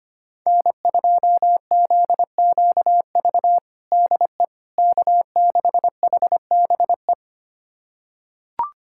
Morse code, heard as a keyed tone, about 25 wpm, 700 Hz. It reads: N2ZQV DE K6HBE
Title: USA Callsign DE Callsign 25wpm